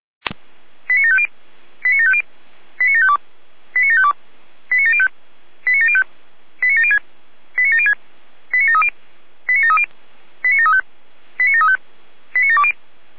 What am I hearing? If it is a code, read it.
Five-tone selective call (ZVEI), heard as quick runs of five tones, tones of about 70 ms. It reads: 89750 89750 89752 89752 89705 89705 89706 89706 89730 89730 89735 89735 89720
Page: /455/SDR/ZVEI/89701/2022.07.13/